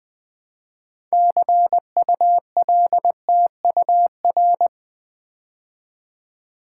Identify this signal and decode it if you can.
Morse code, heard as a keyed tone, about 20 wpm, 700 Hz. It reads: CULTUR
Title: Build A Word - 2 Letters At A Time - 8-Letter Words - 20wpm